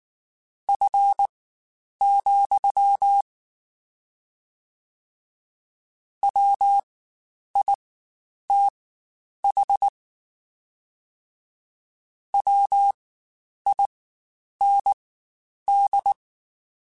Morse code, heard as a keyed tone, about 19 wpm, spaced as for 9 wpm, 800 Hz. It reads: F, WITH WIND
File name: Spaced Characters / Spaced Characters-mp3s / Weather5 22 sp @18